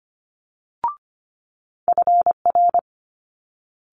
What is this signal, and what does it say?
Morse code, heard as a keyed tone, about 25 wpm, 700 Hz. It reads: FR
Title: Multi-letter Phonemes Quick Fire 25wpm